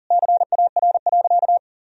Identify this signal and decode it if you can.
Morse code, heard as a keyed tone, about 40 wpm, 700 Hz. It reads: CAR.
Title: Sentences from Top 1000 Words - Review Part 1 - 40wpm